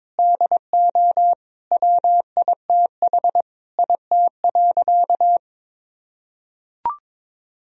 Morse code, heard as a keyed tone, about 22 wpm, 700 Hz. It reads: DO WITH IT.
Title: Sentences from Top 1000 Words - Review Part 3 - 22wpm